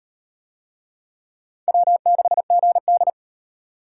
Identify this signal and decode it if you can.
Morse code, heard as a keyed tone, about 34 wpm, 700 Hz. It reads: W6GD